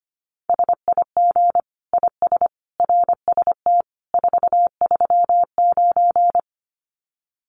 Morse code, heard as a keyed tone, about 25 wpm, 700 Hz. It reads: SIG IS RST 439
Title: Common QSO Phrases 25wpm